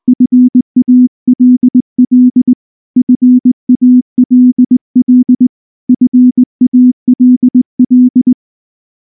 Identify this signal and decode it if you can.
Morse code, heard as a keyed tone, about 19 wpm, 250 Hz. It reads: FALL FALL FALL
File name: morse / morse-training-20WPM-10